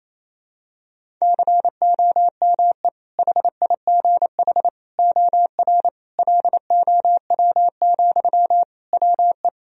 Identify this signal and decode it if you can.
Morse code, heard as a keyed tone, about 28 wpm, 700 Hz. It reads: COME HIGH OR LOW, WE
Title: Sentences from Top 1000 Words - Review Part 1 - 28wpm